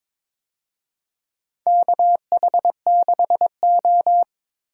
Morse code, heard as a keyed tone, about 23 wpm, 690 Hz. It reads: KH6O
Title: DXCC-100 Call Signs - 4 Characters - 22wpm